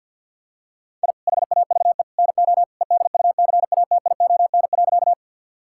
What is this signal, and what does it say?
Morse code, heard as a keyed tone, about 51 wpm, 700 Hz. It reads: I HAVE NO EDUCATION.
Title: Sentences from Top 1000 Words - Review Part 6 - 50wpm